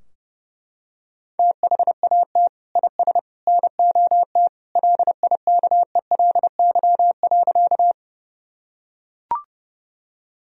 Morse code, heard as a keyed tone, about 32 wpm, 700 Hz. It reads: THAT IS NOT LIKELY.
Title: Sentences from Top 1000 Words - Review Part 4 - 30wpm